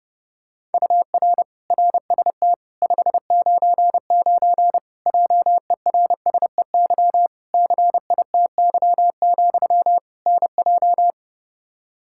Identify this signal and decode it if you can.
Morse code, heard as a keyed tone, about 30 wpm, 700 Hz. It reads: UR RST 599 JERSEY CITY, NJ
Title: Practice QSO - Exchange 1 - Beginning - 30wpm